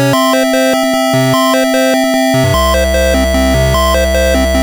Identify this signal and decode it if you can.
There is no code.